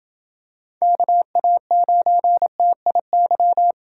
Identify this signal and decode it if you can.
Morse code, heard as a keyed tone, about 28 wpm, 700 Hz. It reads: KA9TIY